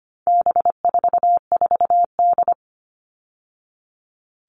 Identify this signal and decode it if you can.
Morse code, heard as a keyed tone, about 25 wpm, 700 Hz. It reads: B44D
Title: All Cut Numbers - 4-Digit 25wpm